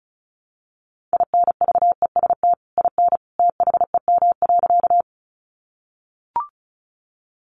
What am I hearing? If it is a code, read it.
Morse code, heard as a keyed tone, about 34 wpm, 700 Hz. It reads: INVEST IN THEM.
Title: Sentences from Top 1100 Words 35wpm